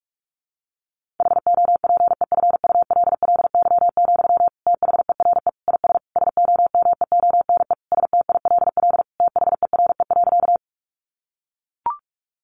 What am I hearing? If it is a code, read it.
Morse code, heard as a keyed tone, about 44 wpm, 700 Hz. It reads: HOPEFULLY, THERE IS SOMEONE STILL THERE.
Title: Sentences from Top 1500 Words 45wpm